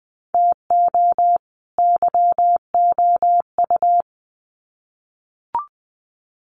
Morse code, heard as a keyed tone, about 20 wpm, 700 Hz. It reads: TO YOU
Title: Sets of 2 Words 20wpm